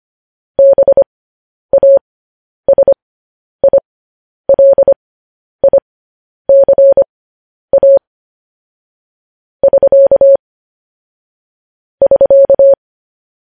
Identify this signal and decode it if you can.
Morse code, heard as a keyed tone, about 25 wpm, spaced as for 10 wpm, 550 Hz. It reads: BASILICA <SK> <SK>
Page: news headlines in morse code at 10 wpm